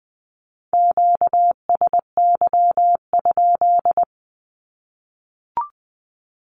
Morse code, heard as a keyed tone, about 20 wpm, 700 Hz. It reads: QSY?
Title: Q-Codes Repeated 3x 20wpm